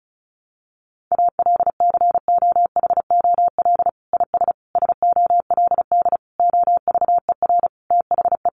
Morse code, heard as a keyed tone, about 34 wpm, 700 Hz. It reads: ALCOHOL IS SOLD OVER THE
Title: Sentences from Top 1400 Words 35wpm